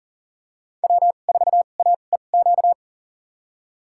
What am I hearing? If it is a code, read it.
Morse code, heard as a keyed tone, about 40 wpm, spaced as for 29 wpm, 700 Hz. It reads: W4AEQ
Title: USA Call Signs - 5 Characters - 2x Character Spacing - 40wpm